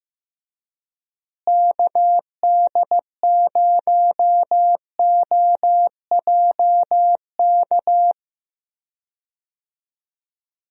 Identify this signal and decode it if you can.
Morse code, heard as a keyed tone, about 15 wpm, 700 Hz. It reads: KD0OJK
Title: USA Call Signs - 6 Characters - Repeated 3x - 15wpm